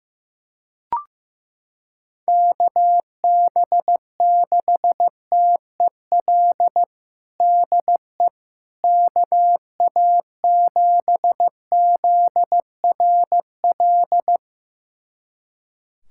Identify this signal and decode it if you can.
Morse code, heard as a keyed tone, about 15 wpm, 700 Hz. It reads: KB6TEL DE KA7ZRL
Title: USA Callsign DE Callsign 15wpm